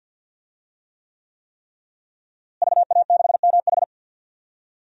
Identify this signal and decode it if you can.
Morse code, heard as a keyed tone, about 50 wpm, 700 Hz. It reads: VA6MH
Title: DXCC-100 Call Signs - 5 Characters - Repeated 3x - 50wpm